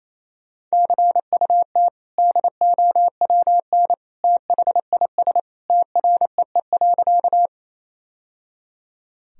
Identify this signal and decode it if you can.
Morse code, heard as a keyed tone, about 28 wpm, 700 Hz. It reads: CUT DOWN THIS TREE.
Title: Sentences from Top 1000 Words - Review Part 1 - 28wpm